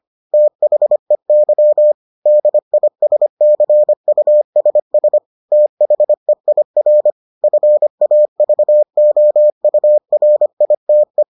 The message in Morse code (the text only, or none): THEY DISCUSS THEIR FAVOURITE